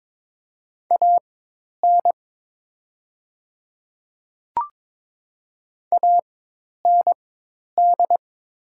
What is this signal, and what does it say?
Morse code, heard as a keyed tone, about 21 wpm, spaced as for 10 wpm, 700 Hz. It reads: AN AND
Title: 4x Character Spacing - Top 100 Words - 22wpm